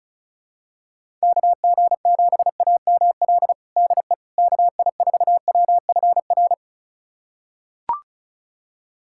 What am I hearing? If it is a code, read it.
Morse code, heard as a keyed tone, about 36 wpm, 700 Hz. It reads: KG7AML DE KI4WFR
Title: USA Callsign DE Callsign 35wpm